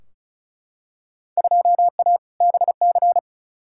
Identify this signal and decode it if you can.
Morse code, heard as a keyed tone, about 34 wpm, 700 Hz. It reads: 2A BC
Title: Field Day Exchanges Encore - 35wpm